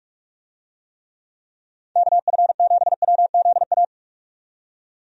Morse code, heard as a keyed tone, about 45 wpm, 700 Hz. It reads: KF7WZA
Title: USA Call Signs - 6 Characters - 45wpm